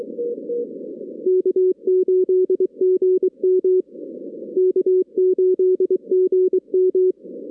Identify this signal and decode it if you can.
Morse code, heard as a keyed tone, about 24 wpm, 370 Hz. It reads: K8GM K8GM
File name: 2016 arrl dx cw / 2016-ARRL-DX-CW-K5TR- - 0-22